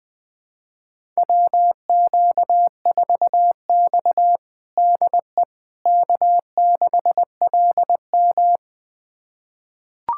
Morse code, heard as a keyed tone, about 20 wpm, 700 Hz. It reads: WQ4X DE K6LM
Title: USA Callsign DE Callsign - 4 Characters - 20wpm